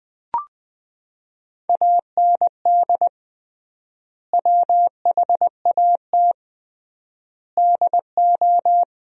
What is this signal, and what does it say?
Morse code, heard as a keyed tone, about 20 wpm, 700 Hz. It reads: AND WHAT DO
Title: 3x Word-Spacing - Sentences from Top 100 Words - 20wpm